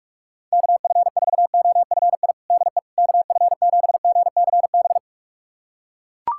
Morse code, heard as a keyed tone, about 42 wpm, 690 Hz. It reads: KF4OFI DE KF7GCB
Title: USA Callsign DE Callsign - 6 Characters - 45wpm